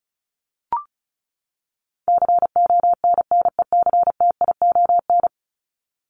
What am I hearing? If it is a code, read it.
Morse code, heard as a keyed tone, about 34 wpm, 700 Hz. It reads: CONNECTION
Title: Top 1001-1250 Words 35wpm